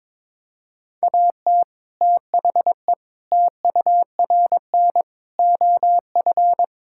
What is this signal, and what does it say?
Morse code, heard as a keyed tone, about 21 wpm, 700 Hz. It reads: AT THE TURN OF
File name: practice / 22wpm / Build A Phrase - 5-Word Phrases - 22wpm